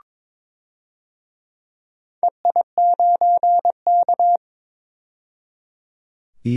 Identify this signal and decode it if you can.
Morse code, heard as a keyed tone, about 21 wpm, 700 Hz. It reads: EI9K
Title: DXCC-100 Call Signs - 4 Characters - 22wpm